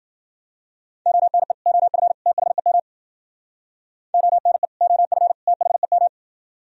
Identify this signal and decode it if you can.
Morse code, heard as a keyed tone, about 50 wpm, 700 Hz. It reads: ONE OF THEM ONE OF THEM
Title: Speed-Racing - Sets of 3 Words 40wpm